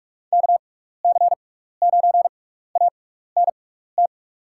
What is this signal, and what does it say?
Morse code, heard as a keyed tone, about 45 wpm, spaced as for 15 wpm, 700 Hz. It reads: KC9ANT
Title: USA Call Signs - 6 Characters - 6x Character Spacing - 45wpm